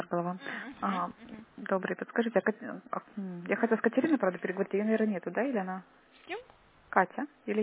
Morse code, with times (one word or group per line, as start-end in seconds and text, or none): none